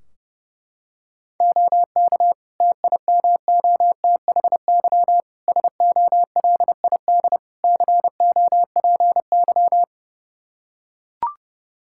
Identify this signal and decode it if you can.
Morse code, heard as a keyed tone, about 32 wpm, 700 Hz. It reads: OK TIMOTHY SOLID COPY
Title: K1USN SST QSO Practice 30wpm